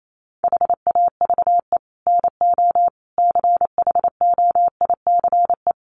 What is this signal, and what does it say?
Morse code, heard as a keyed tone, about 28 wpm, 700 Hz. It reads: HAVE NO CHOICE
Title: Sets of 5 Words Encore 28wpm